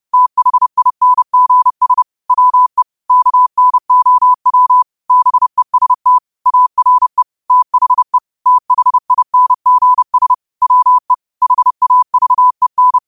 Morse code, heard as a keyed tone, about 30 wpm, 1000 Hz. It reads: THINGS WE KNOW BEST ARE THE THINGS WE HAVEN
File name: qotdfiles / 30wpm